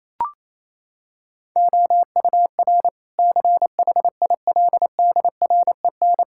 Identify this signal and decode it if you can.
Morse code, heard as a keyed tone, about 28 wpm, 700 Hz. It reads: OUR CHILDREN
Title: Sentences from Top 1000 Words - Review Part 5 - 28wpm